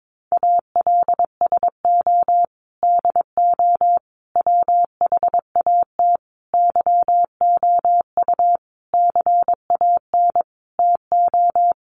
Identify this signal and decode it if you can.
Morse code, heard as a keyed tone, about 22 wpm, 700 Hz. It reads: ALSO DO WHAT YOU CAN TO
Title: Sentences from Top 1300 Words 22wpm